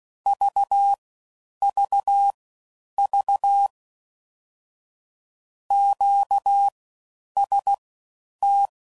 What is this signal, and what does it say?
Morse code, heard as a keyed tone, about 16 wpm, spaced as for 9 wpm, 800 Hz. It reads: VVV QST
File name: Basic Sets / QST de WA7RSO / All Dahs